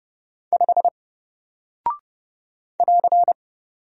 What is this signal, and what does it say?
Morse code, heard as a keyed tone, about 30 wpm, 700 Hz. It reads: 5 <AR>
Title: Single Character (Letter:Number:Punctuation:Prosign) 30wpm